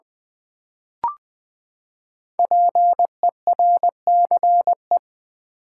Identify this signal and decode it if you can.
Morse code, heard as a keyed tone, about 20 wpm, 700 Hz. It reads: PERCE
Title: Build A Word - 5 Letters At A Time - 10-Letter Words - 20wpm